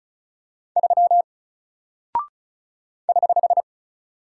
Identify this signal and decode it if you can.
Morse code, heard as a keyed tone, about 36 wpm, 700 Hz. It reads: 3 <HH>
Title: Single Character (Letter:Number:Punctuation:Prosign) Alt 35wpm